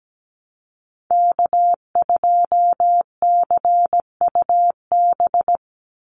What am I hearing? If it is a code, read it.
Morse code, heard as a keyed tone, about 17 wpm, 700 Hz. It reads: K2CUB
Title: Call Signs - 17wpm